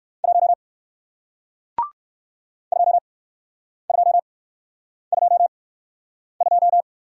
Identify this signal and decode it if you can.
Morse code, heard as a keyed tone, about 56 wpm, spaced as for 9 wpm, 700 Hz. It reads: 1 2222
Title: Speed-Racing Single Letter-Number 45wpm